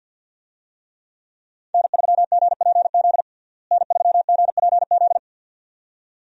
Morse code, heard as a keyed tone, about 50 wpm, 700 Hz. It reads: N3GPZ N3GPZ
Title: USA Call Signs - 5 Characters - Repeated 2x - 50wpm